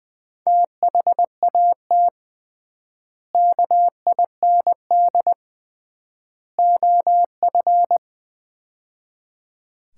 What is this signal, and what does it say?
Morse code, heard as a keyed tone, about 21 wpm, 700 Hz. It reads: THAT KIND OF
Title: 3x Word-Spacing - Sets of 3 Words - 20wpm